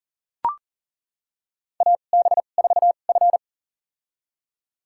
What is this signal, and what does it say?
Morse code, heard as a keyed tone, about 37 wpm, spaced as for 28 wpm, 700 Hz. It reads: AB4F
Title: USA Call Signs - 4 Characters - 2x Character Spacing - 40wpm